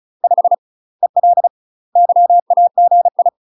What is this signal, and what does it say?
Morse code, heard as a keyed tone, about 36 wpm, 700 Hz. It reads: 5 EL YAGI
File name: UK MASTER QSOs TRAINING / 35WPM / UK MASTER QSOs 0038